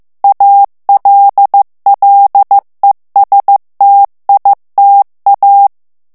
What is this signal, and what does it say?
Morse code, heard as a keyed tone, about 15 wpm, 800 Hz. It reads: ALLESTITA
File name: ITALIEN texte clair 1 vitesse 14 mots minute